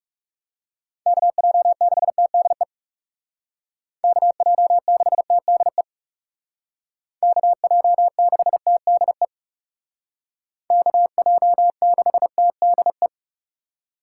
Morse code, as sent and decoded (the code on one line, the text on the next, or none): -.- .--- -.... - -.. / -.- .--- -.... - -.. / -.- .--- -.... - -.. / -.- .--- -.... - -.. .
KJ6TDE KJ6TDE KJ6TDE KJ6TDE